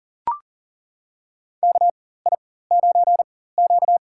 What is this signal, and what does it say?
Morse code, heard as a keyed tone, about 40 wpm, spaced as for 19 wpm, 700 Hz. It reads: KI9Q
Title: USA Call Signs - 4 Characters - 4x Character Spacing - 40wpm